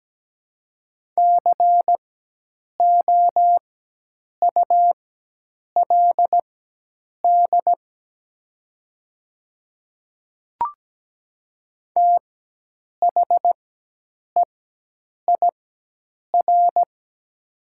Morse code, heard as a keyed tone, about 17 wpm, spaced as for 8 wpm, 700 Hz. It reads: COULD THEIR
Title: 4x Character Spacing - Top 100 Words - 17wpm